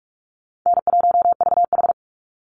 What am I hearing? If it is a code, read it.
Morse code, heard as a keyed tone, about 44 wpm, 700 Hz. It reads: N1VH
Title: Easy Call Signs Repeated 3x - 45wpm